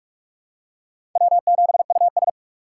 Morse code, heard as a keyed tone, about 44 wpm, 700 Hz. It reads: W7US
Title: USA Call Signs - 4 Characters - 8x Character Spacing - 45wpm